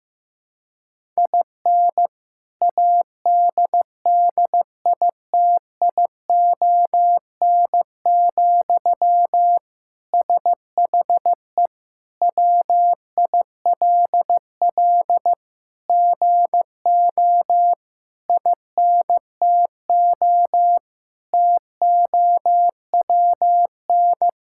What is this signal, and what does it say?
Morse code, heard as a keyed tone, about 15 wpm, 700 Hz. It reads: IN ADDITION, SHE WILL GO INTO TOWN
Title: Sentences from Top 1000 Words - Review Part 6 - 15wpm